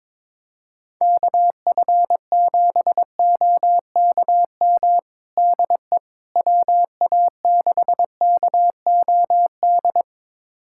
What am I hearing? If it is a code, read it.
Morse code, heard as a keyed tone, about 22 wpm, 700 Hz. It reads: KF7OKM DE WA6KOD
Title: USA Callsign DE Callsign - 6 Characters - 22wpm